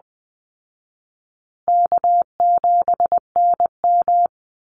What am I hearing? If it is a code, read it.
Morse code, heard as a keyed tone, about 20 wpm, 700 Hz. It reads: K7NM
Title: Easy Call Signs Repeated 2x - 20wpm